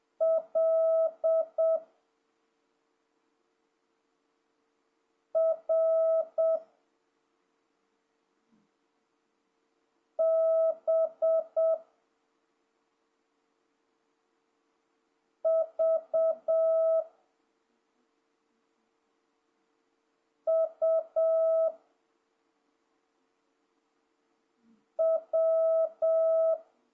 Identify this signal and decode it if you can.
Morse code, heard as a keyed tone, about 7 wpm, 650 Hz. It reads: L R B V U W
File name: jh8cbh / 2026cw